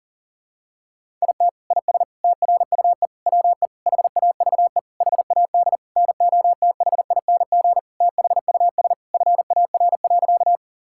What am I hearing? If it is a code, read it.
Morse code, heard as a keyed tone, about 40 wpm, 700 Hz. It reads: IT IS TRUE WE HAVE HAD NOTHING THUS FAR.
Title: Sentences from Top 1000 Words - Review Part 5 - 40wpm